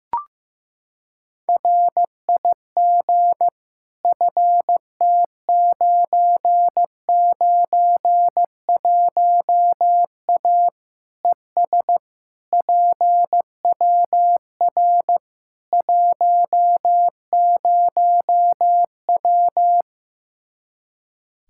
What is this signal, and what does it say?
Morse code, heard as a keyed tone, about 15 wpm, 700 Hz. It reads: RIG FT991A ES PWR 10W
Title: Practice QSO - Exchange 2 - Info - 15wpm